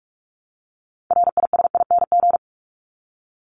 Morse code, heard as a keyed tone, about 44 wpm, 700 Hz. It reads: RISING
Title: ING Suffix - Top 100 Words with Base Words 45wpm